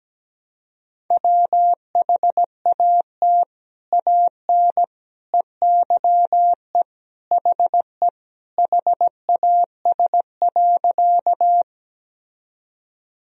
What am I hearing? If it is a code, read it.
Morse code, heard as a keyed tone, about 17 wpm, 700 Hz. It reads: WHAT AN EYE HE HAS.
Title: Sentences from Top 1000 Words - Review Part 1 - 17wpm